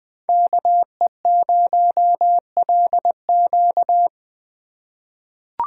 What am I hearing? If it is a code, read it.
Morse code, heard as a keyed tone, about 20 wpm, 700 Hz. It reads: KE0LQ
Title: USA Call Signs - 5 Characters - 4x Character Spacing - 20wpm